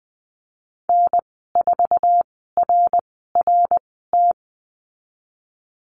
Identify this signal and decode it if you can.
Morse code, heard as a keyed tone, about 20 wpm, spaced as for 15 wpm, 700 Hz. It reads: N4RRT
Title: 055 Lesson 10g - Letter r - Call Signs 20wpm